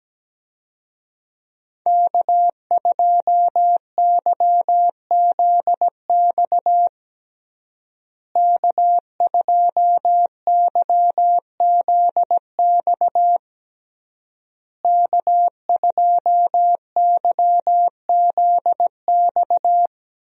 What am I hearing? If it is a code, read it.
Morse code, heard as a keyed tone, about 17 wpm, 700 Hz. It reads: K2YZX K2YZX K2YZX
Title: USA Call Signs - 5 Characters - Repeated 3x - 17wpm